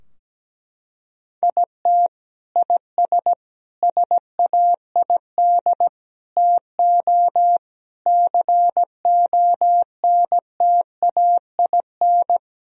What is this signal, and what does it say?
Morse code, heard as a keyed tone, about 17 wpm, 700 Hz. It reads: IT IS SAID TO CONTAIN
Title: Sentences from Top 1000 Words - Review Part 2 - 17wpm